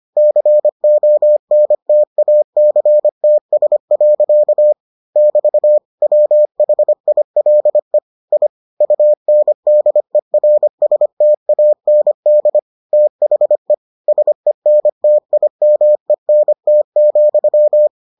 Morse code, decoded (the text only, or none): CONTACTS. <BT> WHILE I UNDERSTAND THE SENTIMENT,